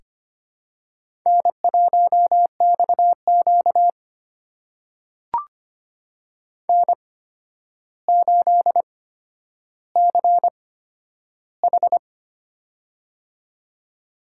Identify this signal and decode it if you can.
Morse code, heard as a keyed tone, about 25 wpm, spaced as for 11 wpm, 700 Hz. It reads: N1XQ N8CH
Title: USA Call Signs - 4 Characters - 8x Character Spacing - 25wpm